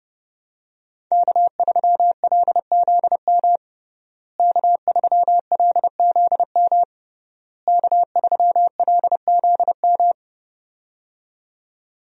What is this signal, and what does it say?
Morse code, heard as a keyed tone, about 30 wpm, 700 Hz. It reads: K3LZM K3LZM K3LZM
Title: USA Call Signs - 5 Characters - Repeated 3x - 30wpm